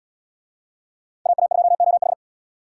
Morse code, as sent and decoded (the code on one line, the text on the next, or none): .- .- ..--- --- ....
AA2OH